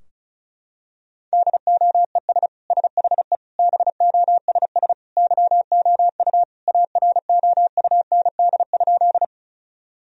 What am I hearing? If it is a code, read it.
Morse code, heard as a keyed tone, about 36 wpm, 700 Hz. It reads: DOES SHE BOSS YOU AROUND?
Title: Sentences from Top 1000 Words - Review Part 4 - 35wpm